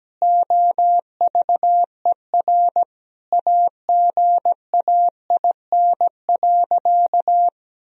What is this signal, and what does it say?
Morse code, heard as a keyed tone, about 17 wpm, 700 Hz. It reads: OVER AGAIN.
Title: Sentences from Top 1000 Words - Review Part 1 - 17wpm